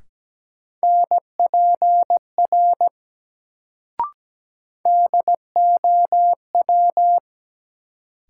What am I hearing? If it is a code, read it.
Morse code, heard as a keyed tone, about 17 wpm, 700 Hz. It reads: NPR DOW
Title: Sets of 5 Words - New Word Drill - 17wpm